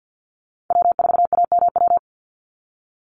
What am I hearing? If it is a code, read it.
Morse code, heard as a keyed tone, about 50 wpm, 700 Hz. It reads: W4AMW
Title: NA SOTA Chaser Callsigns 50wpm